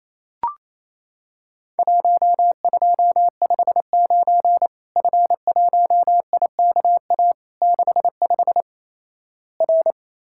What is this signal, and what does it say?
Morse code, heard as a keyed tone, about 28 wpm, 700 Hz. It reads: 1259 F1IKA 65 R
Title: Worked All Europe DX Contest - QTC Exchange - 28wpm